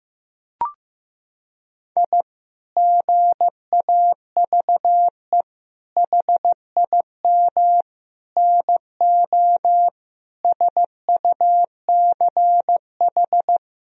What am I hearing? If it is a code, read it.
Morse code, heard as a keyed tone, about 16 wpm, 700 Hz. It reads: I GAVE HIM NO SUCH
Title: Sentences from Top 1000 Words - Review Part 2 - 15wpm